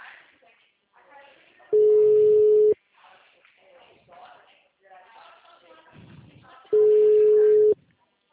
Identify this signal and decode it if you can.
Ringback tone (Europe, 425 Hz), with speech.